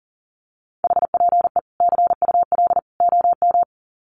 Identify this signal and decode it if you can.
Morse code, heard as a keyed tone, about 40 wpm, 700 Hz. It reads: HPE CUL OM
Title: Common QSO Phrases 40wpm